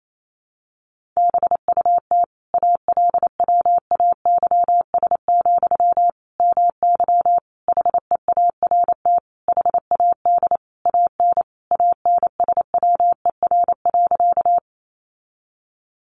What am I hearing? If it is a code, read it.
Morse code, heard as a keyed tone, about 28 wpm, 700 Hz. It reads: BUT ALWAYS, MY HEART HAD AN ANSWER.
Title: Sentences from Top 1200 Words 28wpm